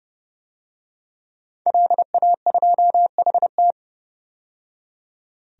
Morse code, heard as a keyed tone, about 32 wpm, 700 Hz. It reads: LA2HT
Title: DXCC-100 Call Signs - 5 Characters - 30wpm